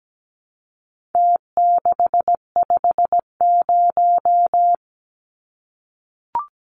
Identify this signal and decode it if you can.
Morse code, heard as a keyed tone, about 17 wpm, 700 Hz. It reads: T650
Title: Numbers - Contest Style - 4-Digit 17wpm